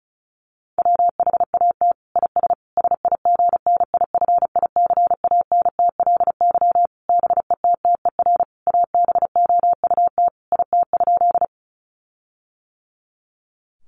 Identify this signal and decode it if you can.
Morse code, heard as a keyed tone, about 34 wpm, 700 Hz. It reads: WHAT IS SIGNIFICANTLY BETTER ABOUT IT?
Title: Sentences from Top 1500 Words 35wpm